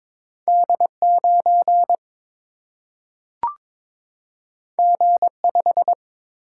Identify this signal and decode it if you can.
Morse code, heard as a keyed tone, about 22 wpm, 700 Hz. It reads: D 9 G 5
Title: Single-Letter Single-Number 22wpm